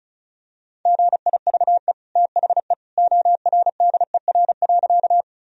Morse code, heard as a keyed tone, about 36 wpm, 700 Hz. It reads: GIVE THE ORDER.
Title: Sentences from Top 1000 Words - Review Part 2 - 35wpm